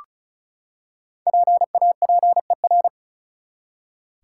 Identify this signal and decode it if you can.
Morse code, heard as a keyed tone, about 36 wpm, 700 Hz. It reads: PAPER
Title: Five Letter Words - 35wpm